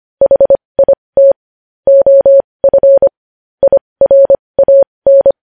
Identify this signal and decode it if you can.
Morse code, heard as a keyed tone, about 25 wpm, spaced as for 20 wpm, 550 Hz. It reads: HIT OF IRAN
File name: podcast / News headlines-1x-25x20